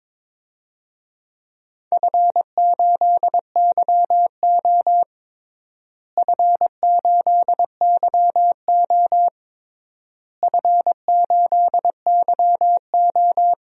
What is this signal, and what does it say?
Morse code, heard as a keyed tone, about 22 wpm, 700 Hz. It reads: F8YO F8YO F8YO
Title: DXCC-100 Call Signs - 4 Characters - Repeated 3x - 22wpm